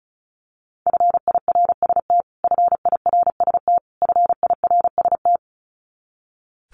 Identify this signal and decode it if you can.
Morse code, heard as a keyed tone, about 34 wpm, 700 Hz. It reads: FIRST FIRST FIRST
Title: Top 300 Words Repeated 35wpm